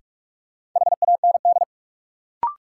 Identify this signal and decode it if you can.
Morse code, heard as a keyed tone, about 46 wpm, 700 Hz. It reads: HAND
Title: Sentences from Top 200 Words - New Word Drill - 45wpm